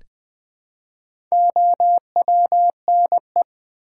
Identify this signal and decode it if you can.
Morse code, heard as a keyed tone, about 20 wpm, 700 Hz. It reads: OWNE
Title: Build A Word - 1 Letter At A Time - 5-Letter Words - 20wpm